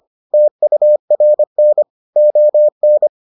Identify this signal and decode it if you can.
Morse code, heard as a keyed tone, about 25 wpm, 600 Hz. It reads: TURN ON